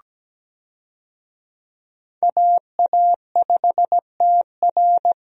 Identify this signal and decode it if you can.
Morse code, heard as a keyed tone, about 17 wpm, 700 Hz. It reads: AA5TR
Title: USA Call Signs - 5 Characters - 17wpm